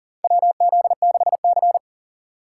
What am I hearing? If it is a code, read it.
Morse code, heard as a keyed tone, about 40 wpm, 700 Hz. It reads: WZ6C
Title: USA Call Signs - 3x Character Spacing - 40wpm